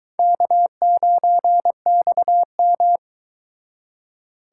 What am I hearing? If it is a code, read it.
Morse code, heard as a keyed tone, about 22 wpm, 700 Hz. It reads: K9XM